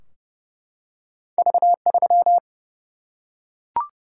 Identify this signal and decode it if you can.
Morse code, heard as a keyed tone, about 32 wpm, 700 Hz. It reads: V3
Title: Single-Letter Single-Number 30wpm